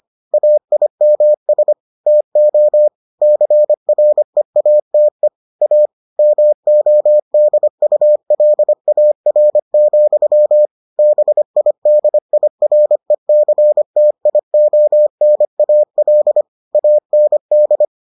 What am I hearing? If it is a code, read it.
Morse code, heard as a keyed tone, about 25 wpm, 600 Hz. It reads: AIMS TO CREATE A MODULAR, BIDIRECTIONAL AND